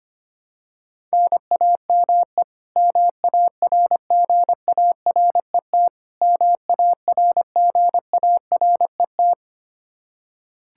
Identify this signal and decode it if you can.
Morse code, heard as a keyed tone, about 25 wpm, 700 Hz. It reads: NAME MARGARET MARGARET
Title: Practice QSO - Exchange 1 - Beginning - 25wpm